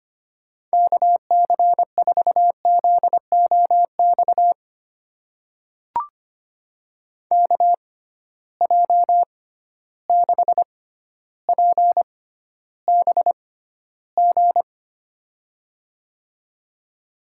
Morse code, heard as a keyed tone, about 25 wpm, spaced as for 13 wpm, 700 Hz. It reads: KC4ZOX KJ6PBG
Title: USA Call Signs - 6 Characters - 6x Character Spacing - 25wpm